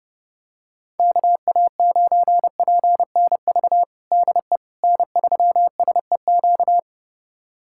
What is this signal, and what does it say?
Morse code, heard as a keyed tone, about 31 wpm, 700 Hz. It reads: KA9PNV DE N3SEQ
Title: USA Callsign DE Callsign 30wpm